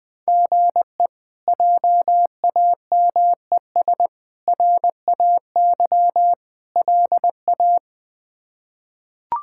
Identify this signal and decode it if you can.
Morse code, heard as a keyed tone, about 20 wpm, 700 Hz. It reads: GE JAMES RAY LA
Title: K1USN SST QSO Practice - 20wpm